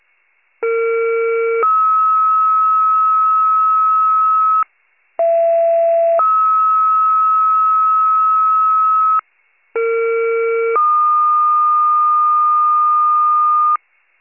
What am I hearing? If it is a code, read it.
Two-tone page, heard as a one-second tone, then a three-second tone, about 460 then 1290 Hz, about 670 then 1290 Hz, about 460 then 1220 Hz. It is heard from 0.6 to 13.8 s.